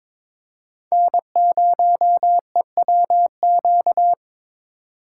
Morse code, heard as a keyed tone, about 22 wpm, 700 Hz. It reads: N0EWQ
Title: USA Call Signs - 2x Character Spacing - 22wpm